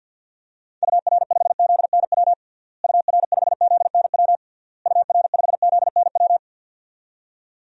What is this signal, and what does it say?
Morse code, heard as a keyed tone, about 52 wpm, 700 Hz. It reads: UR5ZNW UR5ZNW UR5ZNW
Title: DXCC-100 Call Signs - 6 Characters - Repeated 3x - 50wpm